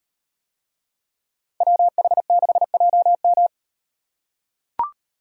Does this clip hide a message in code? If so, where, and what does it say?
Yes: Morse code from 1.6 to 3.5 s, heard as a keyed tone, about 38 wpm, 700 Hz. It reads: WH6JM